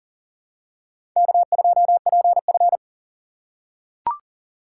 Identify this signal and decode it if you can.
Morse code, heard as a keyed tone, about 40 wpm, 700 Hz. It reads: K2PF
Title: USA Call Signs - 4 Characters - 2x Character Spacing - 40wpm